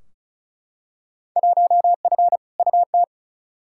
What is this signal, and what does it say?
Morse code, heard as a keyed tone, about 36 wpm, 690 Hz. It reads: 1F UT
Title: Field Day Exchanges Encore - 35wpm